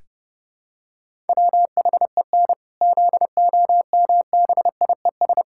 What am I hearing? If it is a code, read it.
Morse code, heard as a keyed tone, about 32 wpm, 700 Hz. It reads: WHEN ZOMBIES
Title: Pangram Challenge v2 - 30wpm